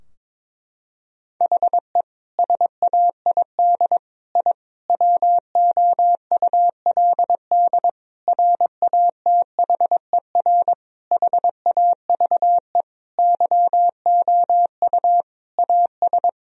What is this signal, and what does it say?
Morse code, heard as a keyed tone, about 22 wpm, 700 Hz. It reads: HE SAID I WOULD RATHER HAVE YOU AS
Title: Sentences from Top 1000 Words - Review Part 4 - 22wpm